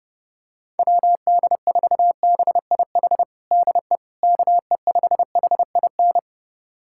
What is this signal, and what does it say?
Morse code, heard as a keyed tone, about 30 wpm, 700 Hz. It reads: WD4BIH DE KE5HIN
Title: USA Callsign DE Callsign 30wpm